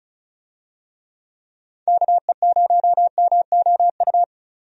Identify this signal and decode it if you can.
Morse code, heard as a keyed tone, about 34 wpm, 700 Hz. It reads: KE0MOU
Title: USA Call Signs - 6 Characters - 35wpm